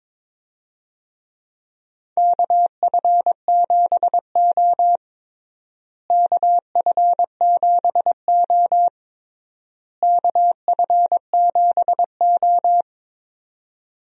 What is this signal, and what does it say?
Morse code, heard as a keyed tone, about 22 wpm, 700 Hz. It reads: KF7O KF7O KF7O
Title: USA Call Signs - 4 Characters - Repeated 3x - 22wpm